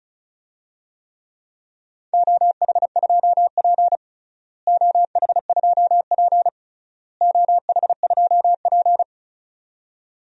Morse code, heard as a keyed tone, about 34 wpm, 700 Hz. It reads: OH2P OH2P OH2P
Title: DXCC-100 Call Signs - 4 Characters - Repeated 3x - 35wpm